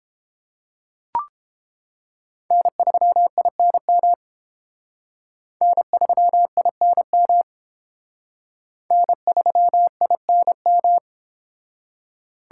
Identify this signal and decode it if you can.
Morse code, heard as a keyed tone, about 30 wpm, 700 Hz. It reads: N3INM N3INM N3INM